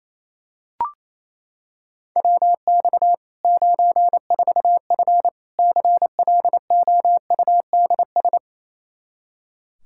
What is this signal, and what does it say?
Morse code, heard as a keyed tone, about 28 wpm, 700 Hz. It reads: WX 94F CLOUDS
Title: Practice QSO - Exchange 2 - Info - 28wpm